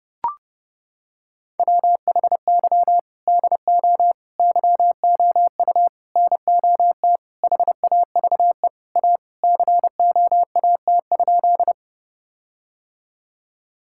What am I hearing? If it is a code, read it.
Morse code, heard as a keyed tone, about 30 wpm, 700 Hz. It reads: WHY DO YOU NOT HAVE A COAT?
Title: Sentences from Top 1600 Words 30wpm